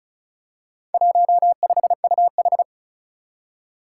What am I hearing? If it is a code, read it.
Morse code, heard as a keyed tone, about 36 wpm, 700 Hz. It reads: 15UH
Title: Four Challenging Letter-Number Combinations 35wpm